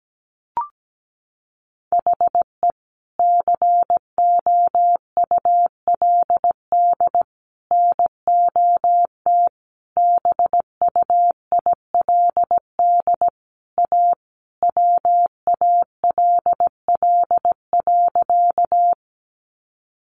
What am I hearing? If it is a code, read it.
Morse code, heard as a keyed tone, about 17 wpm, 700 Hz. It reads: HE COULD NOT BUILD A WALL.
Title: Sentences from Top 1100 Words 17wpm